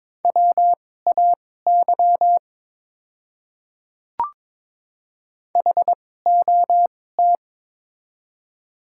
Morse code, heard as a keyed tone, about 22 wpm, spaced as for 16 wpm, 690 Hz. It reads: WAY HOT
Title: 2x Character Spacing - Top 100 Words - 22wpm